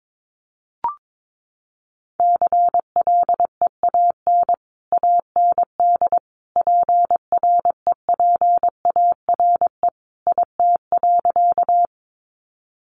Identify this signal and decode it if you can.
Morse code, heard as a keyed tone, about 22 wpm, 700 Hz. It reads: CLEAN AND PREPARE IT.
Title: Sentences from Top 1100 Words 22wpm